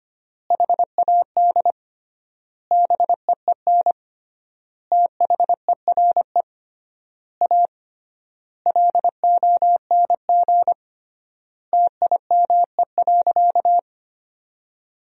Morse code, heard as a keyed tone, about 25 wpm, 700 Hz. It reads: HAD BEEN THERE A LONG TIME.
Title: 3x Word-Spacing - Sentences from Top 400 Words - 25wpm